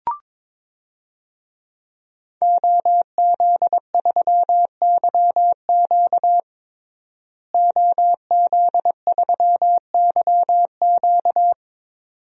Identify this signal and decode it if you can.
Morse code, heard as a keyed tone, about 22 wpm, 700 Hz. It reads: OZ3YQ OZ3YQ
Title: DXCC-100 Call Signs - 5 Characters - Repeated 2x - 22wpm